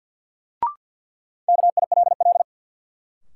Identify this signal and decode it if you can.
Morse code, heard as a keyed tone, about 53 wpm, 700 Hz. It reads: KILL
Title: Sentences from Top 1400 Words - New Word Drill - 50wpm